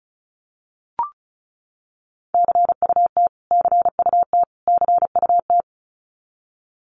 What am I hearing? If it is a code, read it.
Morse code, heard as a keyed tone, about 34 wpm, 700 Hz. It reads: CUT CUT CUT
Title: Top 500 Words Repeated 35wpm